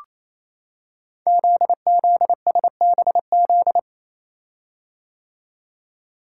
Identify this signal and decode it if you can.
Morse code, heard as a keyed tone, about 29 wpm, 700 Hz. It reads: ZZSBZ
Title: Code Groups - 5 Letters - 28wpm